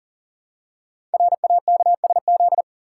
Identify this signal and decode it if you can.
Morse code, heard as a keyed tone, about 42 wpm, 700 Hz. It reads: RAKSZ
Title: Code Groups - 5 Letter:Numbers - 40wpm